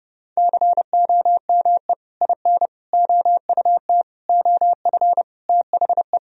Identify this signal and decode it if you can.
Morse code, heard as a keyed tone, about 30 wpm, 700 Hz. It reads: COME IN OUT OF THE
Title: Sentences from Top 1000 Words - Review Part 1 - 30wpm